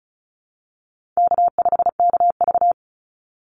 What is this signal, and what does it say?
Morse code, heard as a keyed tone, about 34 wpm, 700 Hz. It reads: K5KV
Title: Call Signs Repeated 2x - 35wpm